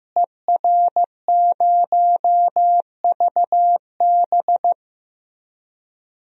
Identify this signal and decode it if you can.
Morse code, heard as a keyed tone, about 15 wpm, 700 Hz. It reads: ER0VB
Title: DXCC-100 Call Signs - 5 Characters - Repeated 3x - 15wpm